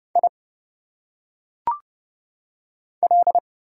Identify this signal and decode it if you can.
Morse code, heard as a keyed tone, about 30 wpm, 700 Hz. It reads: I L
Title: Single Letters - Phonetic Pronunciation 30wpm